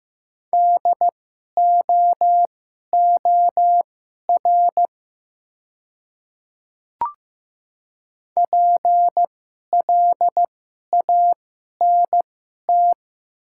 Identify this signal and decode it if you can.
Morse code, heard as a keyed tone, about 15 wpm, spaced as for 11 wpm, 700 Hz. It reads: DOOR PLANT
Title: 2x Character Spacing - Top 300 Words - 15wpm